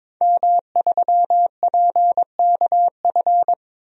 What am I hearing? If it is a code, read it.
Morse code, heard as a keyed tone, about 22 wpm, 700 Hz. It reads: M3PKF
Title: DXCC-100 Call Signs - 5 Characters - Repeated 3x - 22wpm